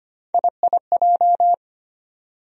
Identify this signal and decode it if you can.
Morse code, heard as a keyed tone, about 25 wpm, 700 Hz. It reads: IIJ
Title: USA Call Sign Postfixes - 3 Letters 25wpm